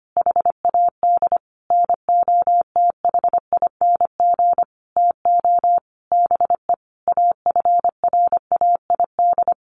Morse code, heard as a keyed tone, about 25 wpm, 700 Hz. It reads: HAD NOTHING TO BE AFRAID
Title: Sentences from Top 1500 Words 25wpm